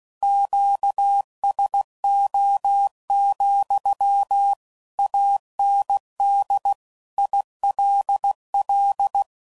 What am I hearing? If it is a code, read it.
Morse code, heard as a keyed tone, about 16 wpm, 800 Hz. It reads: QSO, AND ILL
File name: CW Code Groups @15WPM / QSOs / kc7sac